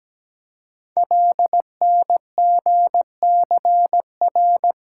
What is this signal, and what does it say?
Morse code, heard as a keyed tone, about 17 wpm, 700 Hz. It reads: LNGCR
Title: Code Groups - 5 Letters - 17wpm